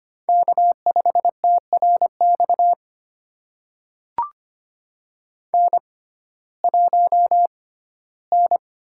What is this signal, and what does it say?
Morse code, heard as a keyed tone, about 25 wpm, spaced as for 15 wpm, 700 Hz. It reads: K5TRX N1N
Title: USA Call Signs - 5 Characters - 6x Character Spacing - 25wpm